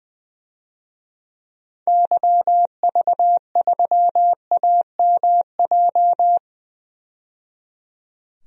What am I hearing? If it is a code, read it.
Morse code, heard as a keyed tone, about 20 wpm, 700 Hz. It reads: YV3AMJ
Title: DXCC-100 Call Signs - 6 Characters - 20wpm